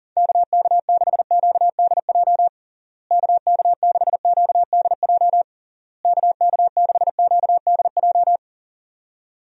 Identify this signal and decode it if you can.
Morse code, heard as a keyed tone, about 40 wpm, 700 Hz. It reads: KK6QDJ KK6QDJ KK6QDJ
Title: USA Call Signs - 6 Characters - Repeated 3x - 40wpm